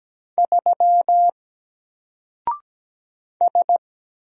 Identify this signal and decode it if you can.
Morse code, heard as a keyed tone, about 17 wpm, 700 Hz. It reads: 3 S
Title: Single Character (Letter:Number:Punctuation:Prosign) 17wpm